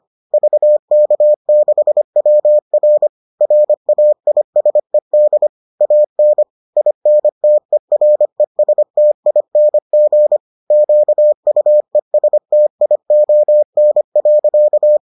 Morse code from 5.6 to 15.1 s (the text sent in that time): AN INTERESTING QUESTION.